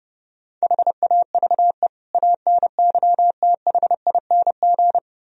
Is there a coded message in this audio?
Yes: Morse code from 0.6 to 5.0 s, heard as a keyed tone, about 40 wpm, 700 Hz. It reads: HAVE ANYTHING